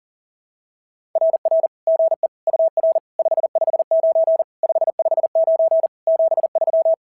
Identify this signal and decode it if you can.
Morse code, heard as a keyed tone, about 40 wpm, 690 Hz. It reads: RR GE UR 559 559 73
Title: SOTA QSO Practice 40wpm